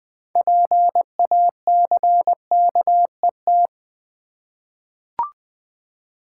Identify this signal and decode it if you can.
Morse code, heard as a keyed tone, about 20 wpm, 700 Hz. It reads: PACKET
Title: 6 Letter Words - First 3 Letters Given - 20wpm